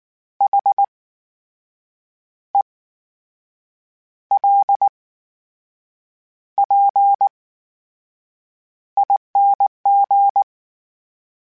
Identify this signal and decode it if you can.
Morse code, heard as a keyed tone, about 19 wpm, spaced as for 6 wpm, 800 Hz. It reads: HELPING
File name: Special INGs / Special INGs-mp3s / VerySlowToFaster Ing @18